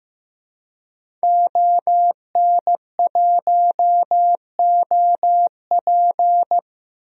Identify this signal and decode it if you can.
Morse code, heard as a keyed tone, about 16 wpm, 700 Hz. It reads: ON1OP
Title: DXCC-100 Call Signs - 5 Characters - 15wpm